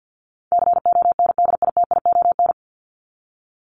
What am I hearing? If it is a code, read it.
Morse code, heard as a keyed tone, about 50 wpm, 700 Hz. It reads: CONDITION
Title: Top 751-1000 Words 50wpm